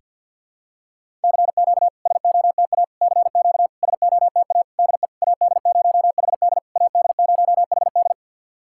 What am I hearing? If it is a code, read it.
Morse code, heard as a keyed tone, about 48 wpm, 690 Hz. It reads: CQ SOTA CQ SOTA DE AD0HD AD0HD
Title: SOTA QSO Practice 50wpm